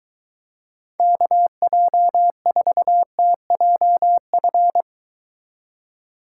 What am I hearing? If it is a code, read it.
Morse code, heard as a keyed tone, about 23 wpm, 700 Hz. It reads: KJ4TJF